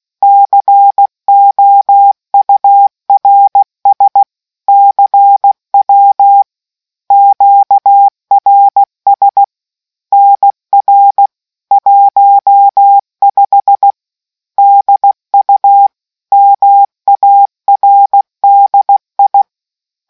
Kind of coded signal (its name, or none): Morse code